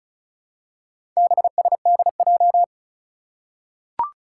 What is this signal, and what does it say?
Morse code, heard as a keyed tone, about 36 wpm, 700 Hz. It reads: BSDJ
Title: Four Challenging Letter-Number Combinations 35wpm